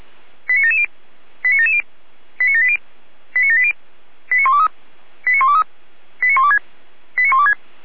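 Five-tone selective call (ZVEI), heard as quick runs of five tones, tones of about 70 ms. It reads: 89770 89770 89780 89780 89123 89123 89126 89126
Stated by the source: ZVEI five-tone call received on an SDR.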